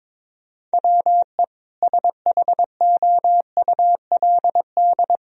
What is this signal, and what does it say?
Morse code, heard as a keyed tone, about 21 wpm, 700 Hz. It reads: WE SHOULD
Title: Sentences from Top 1000 Words - Review Part 3 - 22wpm